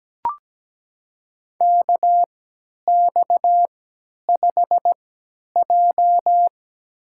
Morse code, heard as a keyed tone, about 17 wpm, spaced as for 10 wpm, 700 Hz. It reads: KX5J
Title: USA Call Signs - 3x Character Spacing - 17wpm